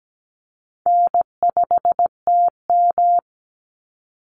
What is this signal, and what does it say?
Morse code, heard as a keyed tone, about 17 wpm, 700 Hz. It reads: N5TM
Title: Easy Call Signs - 17wpm